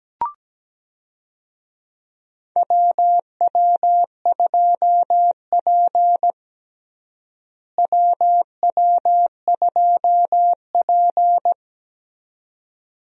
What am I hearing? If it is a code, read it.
Morse code, heard as a keyed tone, about 17 wpm, 700 Hz. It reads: WW2P WW2P
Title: USA Call Signs - 4 Characters - Repeated 2x - 17wpm